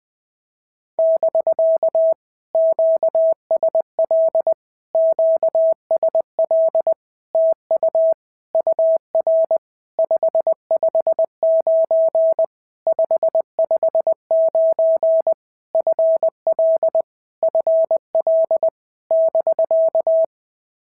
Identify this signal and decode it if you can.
Morse code, heard as a keyed tone, about 20 wpm, 650 Hz. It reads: <BK> QSL QSL TU UR 559 559 FL FL <BK>
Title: POTA QSO Practice - 20wpm